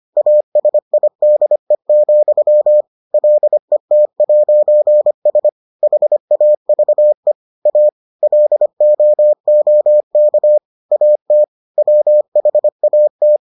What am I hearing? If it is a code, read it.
Morse code, heard as a keyed tone, about 25 wpm, 600 Hz. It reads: ASIDE, LET'S HAVE A LOOK AT WHAT